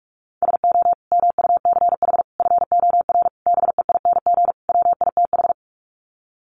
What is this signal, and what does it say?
Morse code, heard as a keyed tone, about 44 wpm, 700 Hz. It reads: SO MUCH FOR BEING WITH
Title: Sets of 5 Words Encore 45wpm